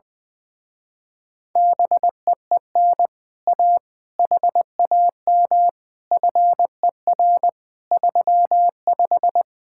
Morse code, heard as a keyed tone, about 20 wpm, 700 Hz. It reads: BEEN A HAM FER 35
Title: Speed-Racing Common QSO Phrases - 20wpm